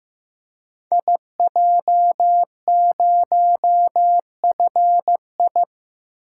Morse code, heard as a keyed tone, about 15 wpm, 700 Hz. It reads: IJ0FI
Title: Code Groups - 5 Letter:Numbers - 15wpm